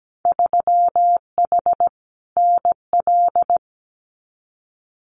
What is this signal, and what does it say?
Morse code, heard as a keyed tone, about 17 wpm, 700 Hz. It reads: 3H NL
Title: Winter Field Day Exchanges Encore 17wpm